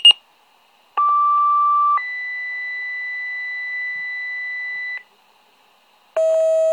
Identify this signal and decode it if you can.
Two-tone page, heard as a one-second tone, then a three-second tone, about 1170 then 2000 Hz, with noise. It is heard from 1.0 to 5.0 s.
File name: pager alerting